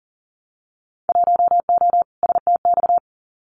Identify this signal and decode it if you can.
Morse code, heard as a keyed tone, about 40 wpm, 700 Hz. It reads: 1O STX
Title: Winter Field Day Exchanges 40wpm